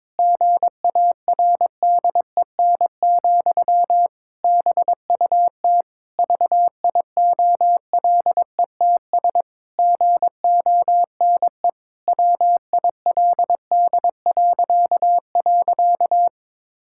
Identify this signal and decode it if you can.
Morse code, heard as a keyed tone, about 22 wpm, 700 Hz. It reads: GARDEN, BUT VIOLETS GONE WILD..
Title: Pangram Challenge v2 - 22wpm